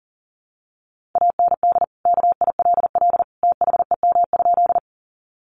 Morse code, heard as a keyed tone, about 37 wpm, 700 Hz. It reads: AND KILL THEM?
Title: Sentences from Top 1400 Words 40wpm